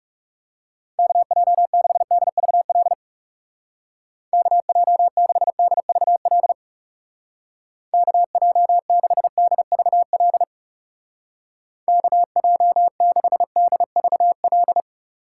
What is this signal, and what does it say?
Morse code, heard as a keyed tone, about 38 wpm, 700 Hz. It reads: KJ6DVL KJ6DVL KJ6DVL KJ6DVL